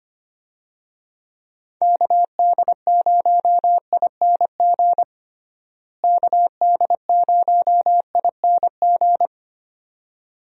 Morse code, heard as a keyed tone, about 25 wpm, 700 Hz. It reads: KD0ING KD0ING
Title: USA Call Signs - 6 Characters - Repeated 2x - 25wpm